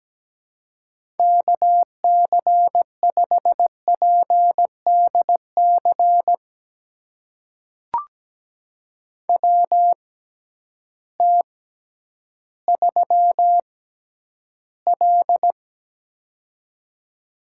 Morse code, heard as a keyed tone, about 17 wpm, spaced as for 10 wpm, 700 Hz. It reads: KC5PDC WT3L
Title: USA Call Signs - 6x Character Spacing - 17wpm